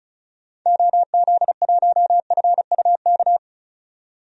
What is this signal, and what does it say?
Morse code, heard as a keyed tone, about 34 wpm, 700 Hz. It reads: OZ1FUK
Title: DXCC-100 Call Signs - 6 Characters - 35wpm